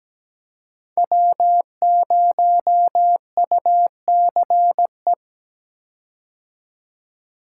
Morse code, heard as a keyed tone, about 17 wpm, 700 Hz. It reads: W0UCE
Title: K1USN SST QSO Practice - 17wpm